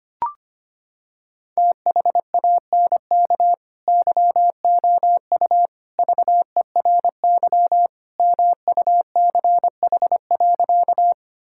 Morse code, heard as a keyed tone, about 25 wpm, 700 Hz. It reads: THANK YOU VERY MUCH.
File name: practice / 25wpm / Sentences with 3-Word Phrases - 25wpm